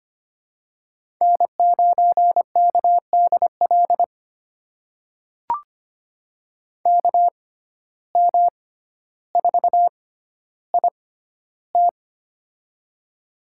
Morse code, heard as a keyed tone, about 25 wpm, spaced as for 13 wpm, 700 Hz. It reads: N9KDL KM4IT
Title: USA Call Signs - 5 Characters - 6x Character Spacing - 25wpm